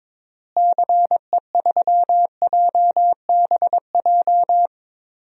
Morse code, heard as a keyed tone, about 21 wpm, 700 Hz. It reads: CE3JBJ
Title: DXCC-100 Call Signs - 6 Characters - Repeated 2x - 22wpm